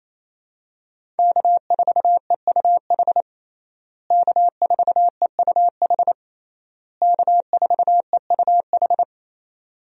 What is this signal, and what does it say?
Morse code, heard as a keyed tone, about 28 wpm, 700 Hz. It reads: K4EUH K4EUH K4EUH
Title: USA Call Signs - 5 Characters - Repeated 3x - 28wpm